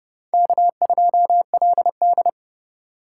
Morse code, heard as a keyed tone, about 30 wpm, 700 Hz. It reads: K2LD
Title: USA Call Signs - 4 Characters - 2x Character Spacing - 30wpm